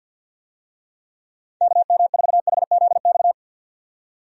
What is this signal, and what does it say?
Morse code, heard as a keyed tone, about 52 wpm, 700 Hz. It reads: KM4HZX
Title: USA Call Signs - 6 Characters - 50wpm